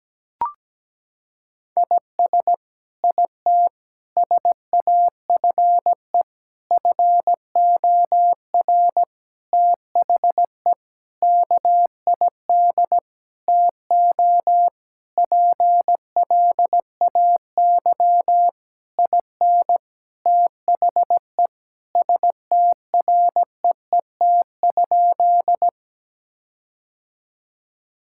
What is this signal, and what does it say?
Morse code, heard as a keyed tone, about 17 wpm, 700 Hz. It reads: IS IT SAFE FOR THE KID TO PLAY IN THE STREET?
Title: Sentences from Top 1600 Words 17wpm